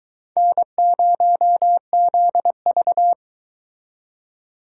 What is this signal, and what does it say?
Morse code, heard as a keyed tone, about 24 wpm, 700 Hz. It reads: N0ZV